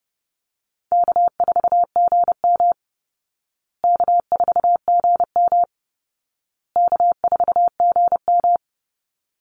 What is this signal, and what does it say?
Morse code, heard as a keyed tone, about 30 wpm, 700 Hz. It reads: K4GM K4GM K4GM
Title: Easy Call Signs Repeated 3x - 30wpm